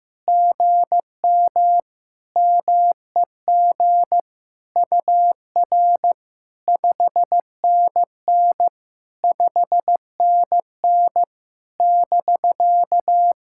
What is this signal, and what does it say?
Morse code, heard as a keyed tone, about 16 wpm, 700 Hz. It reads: GM MEG UR 5NN 5NN <BK>
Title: SOTA QSO Practice 15wpm